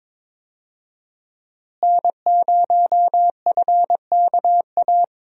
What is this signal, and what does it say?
Morse code, heard as a keyed tone, about 22 wpm, 700 Hz. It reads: N0FKA
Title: USA Call Signs - 5 Characters - Repeated 3x - 22wpm